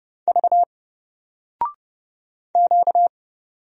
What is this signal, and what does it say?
Morse code, heard as a keyed tone, about 30 wpm, 700 Hz. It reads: V Q
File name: Single-Letter-Number-30wpm